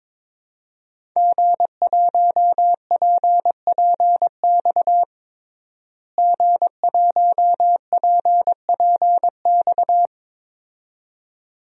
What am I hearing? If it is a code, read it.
Morse code, heard as a keyed tone, about 22 wpm, 700 Hz. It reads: G1PPX G1PPX
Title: DXCC-100 Call Signs - 5 Characters - Repeated 2x - 22wpm